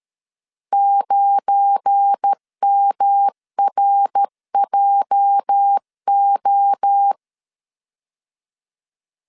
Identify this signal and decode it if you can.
Morse code, heard as a keyed tone, about 13 wpm, 800 Hz. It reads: 9MRJO
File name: LEZIONI / 8 YJ45 / 8 YJ 12wpm 2235